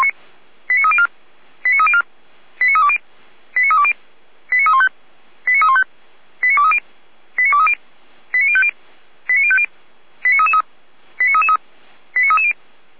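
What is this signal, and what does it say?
Five-tone selective call (ZVEI), heard as quick runs of five tones, tones of about 80 ms. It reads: 89304 89304 89320 89320 89315 89315 89230 89230 89050 89050 89333 89330
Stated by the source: ZVEI five-tone selective call 89303, received on SDR.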